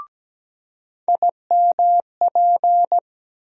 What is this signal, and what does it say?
Morse code, heard as a keyed tone, about 17 wpm, 700 Hz. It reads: IMP
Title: USA Call Sign Postfixes - 3 Letters 17wpm